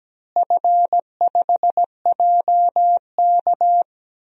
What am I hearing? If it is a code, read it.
Morse code, heard as a keyed tone, about 17 wpm, 700 Hz. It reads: F5JK
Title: DXCC-100 Call Signs - 4 Characters - Repeated 3x - 17wpm